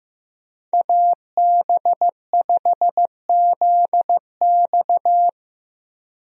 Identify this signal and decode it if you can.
Morse code, heard as a keyed tone, about 15 wpm, 700 Hz. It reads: AB5ZX
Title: USA Call Signs - 5 Characters - Repeated 3x - 15wpm